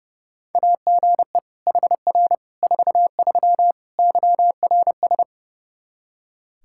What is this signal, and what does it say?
Morse code, heard as a keyed tone, about 30 wpm, 700 Hz. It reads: AGE HR 43 YRS
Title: Practice QSO - Exchange 3 - More Info - 30wpm